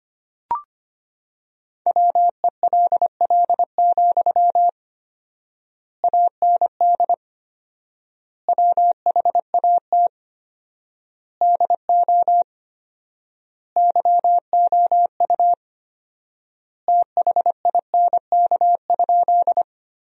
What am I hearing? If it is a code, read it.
Morse code, heard as a keyed tone, about 25 wpm, 700 Hz. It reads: WELL, AND WHAT DO YOU THINK?
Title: 4x Word-Spacing - Sentences from Top 200 Words - 25wpm